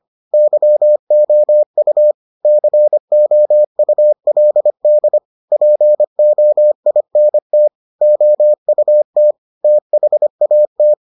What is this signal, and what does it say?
Morse code, heard as a keyed tone, about 25 wpm, 600 Hz. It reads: YOU COULD POINT OUT THAT